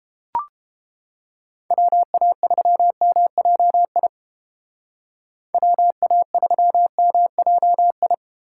Morse code, heard as a keyed tone, about 33 wpm, 700 Hz. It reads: WA3MJI WA3MJI